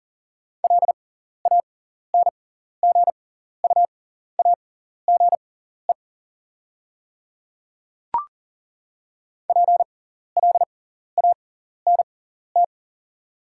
Morse code, heard as a keyed tone, about 40 wpm, spaced as for 14 wpm, 690 Hz. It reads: LANGUAGE PLANT
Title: 6x Character Spacing - Top 500 Words - 40wpm